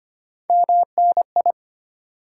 Morse code, heard as a keyed tone, about 25 wpm, 700 Hz. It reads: MNI
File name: Top 90 QSO Elements - 2x Character Spacing - 25wpm